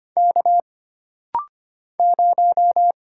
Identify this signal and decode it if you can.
Morse code, heard as a keyed tone, about 25 wpm, 700 Hz. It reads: K 0
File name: Single Character (Letter:Number:Punctuation:Prosign) - Rapid-Fire - 25wpm